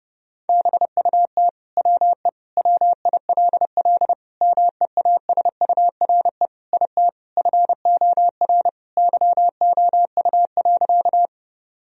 Morse code, heard as a keyed tone, about 30 wpm, 700 Hz. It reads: BUT WE WILL MEASURE IT FOR YOU.
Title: Sentences from Top 1000 Words - Review Part 2 - 30wpm